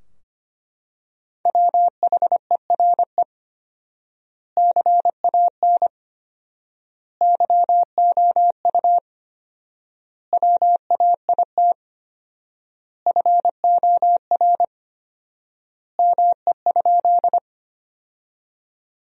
Morse code, heard as a keyed tone, about 25 wpm, 700 Hz. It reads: WHERE CAN YOU WAIT FOR ME?
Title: 4x Word-Spacing - Sentences from Top 500 Words - 25wpm